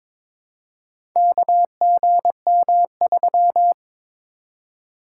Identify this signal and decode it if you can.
Morse code, heard as a keyed tone, about 22 wpm, 700 Hz. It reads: KGM3
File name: practice / 22wpm / Four Letter-Number Combinations - 22wpm